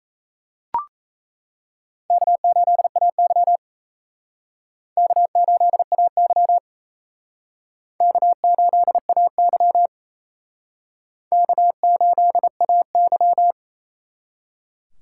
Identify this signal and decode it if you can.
Morse code, heard as a keyed tone, about 35 wpm, 700 Hz. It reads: K8AY K8AY K8AY K8AY